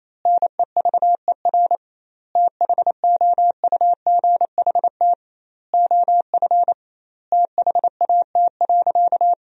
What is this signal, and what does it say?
Morse code, heard as a keyed tone, about 28 wpm, 700 Hz. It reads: NEVER THOUGHT OF THAT.
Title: 2x Word-Spacing - Sentences from Top 300 Words - 28wpm